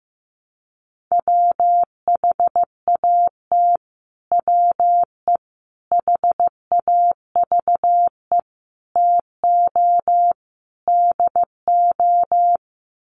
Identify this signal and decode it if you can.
Morse code, heard as a keyed tone, about 15 wpm, 700 Hz. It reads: WHAT WE HAVE TO DO
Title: Sets of 5 Words Encore 15wpm